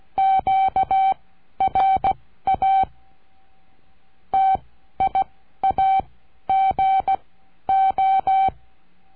Morse code, heard as a keyed tone, about 16 wpm, spaced as for 12 wpm, 770 Hz. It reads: QRA TIAGO
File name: CURSO DE TELEGRAFIA